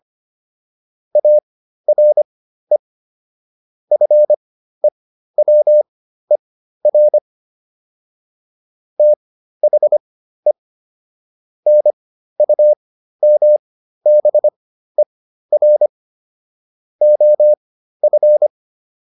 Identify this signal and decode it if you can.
Morse code, heard as a keyed tone, about 25 wpm, spaced as for 13 wpm, 600 Hz. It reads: ARE FEWER THE NUMBER OF